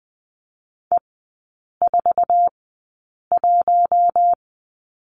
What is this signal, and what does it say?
Morse code, heard as a keyed tone, about 19 wpm, spaced as for 8 wpm, 700 Hz. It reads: E41
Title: 067 Lesson 12e - Letter d - Three Characters 20wpm